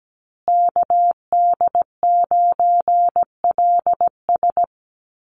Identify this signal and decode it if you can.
Morse code, heard as a keyed tone, about 17 wpm, 700 Hz. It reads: KD9LS
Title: USA Call Signs Repeated 2x 17wpm